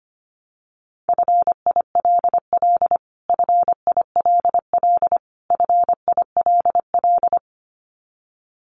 Morse code, heard as a keyed tone, about 25 wpm, 700 Hz. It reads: FILL FILL FILL
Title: Top 500 Words Repeated 25wpm